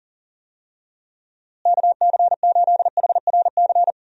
Morse code, heard as a keyed tone, about 40 wpm, 700 Hz. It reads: KC8HRC
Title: USA Call Signs - 6 Characters - Repeated 3x - 40wpm